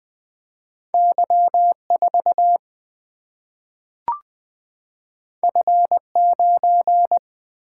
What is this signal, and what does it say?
Morse code, heard as a keyed tone, about 20 wpm, 700 Hz. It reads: Y4 F9
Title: Single-Letter Single-Number 20wpm